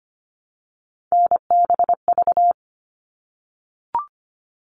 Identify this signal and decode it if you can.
Morse code, heard as a keyed tone, about 25 wpm, 700 Hz. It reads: NBV
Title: All Cut Numbers - 3-Digit 25wpm